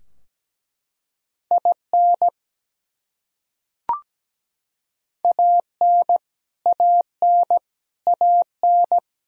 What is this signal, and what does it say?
Morse code, heard as a keyed tone, about 17 wpm, 700 Hz. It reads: IN AN AN AN
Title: Top 100 Words Repeated 17wpm